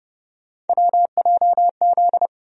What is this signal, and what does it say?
Morse code, heard as a keyed tone, about 30 wpm, 700 Hz. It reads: WJZ
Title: USA Call Sign Postfixes - 3 Letters 30wpm